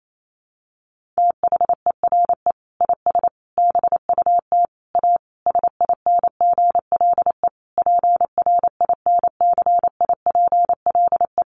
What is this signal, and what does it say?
Morse code, heard as a keyed tone, about 28 wpm, 700 Hz. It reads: THERE IS BUT A SINGLE PRINCIPLE
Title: Sentences from Top 1400 Words 28wpm